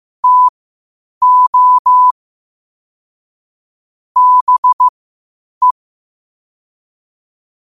Morse code, heard as a keyed tone, about 15 wpm, spaced as for 8 wpm, 1000 Hz. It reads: TO BE
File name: qotdfiles / 8wpm